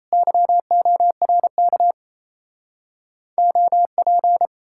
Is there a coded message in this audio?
Yes: Morse code from 0.1 to 4.5 s, heard as a keyed tone, about 38 wpm, 700 Hz. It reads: YORK OP